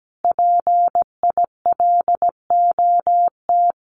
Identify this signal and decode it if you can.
Morse code, heard as a keyed tone, about 17 wpm, 700 Hz. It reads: PILOT
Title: Common QSO Phrases Encore 17wpm